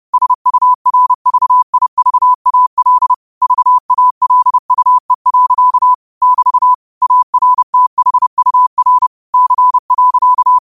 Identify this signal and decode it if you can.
Morse code, heard as a keyed tone, about 30 wpm, 1000 Hz. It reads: SURVIVAL VALUE. <BT> ARTHUR C.
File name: qotdfiles / 30wpm